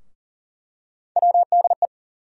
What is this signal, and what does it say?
Morse code, heard as a keyed tone, about 40 wpm, 700 Hz. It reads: WDE
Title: USA Call Sign Postfixes - 3 Letters 40wpm